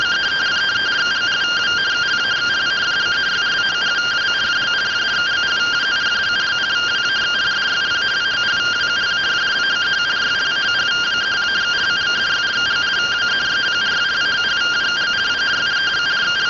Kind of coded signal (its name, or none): two-tone teleprinter signal